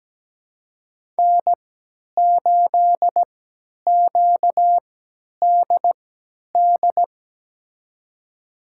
Morse code, heard as a keyed tone, about 17 wpm, spaced as for 10 wpm, 700 Hz. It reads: N8QDD
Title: USA Call Signs - 5 Characters - 3x Character Spacing - 17wpm